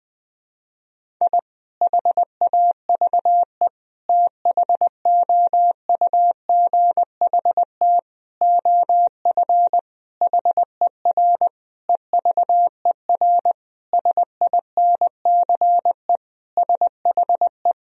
Morse code, heard as a keyed tone, about 20 wpm, 700 Hz. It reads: I HAVE THOUGHT OF HER EVER SINCE SHE
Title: Sentences from Top 1000 Words - Review Part 1 - 20wpm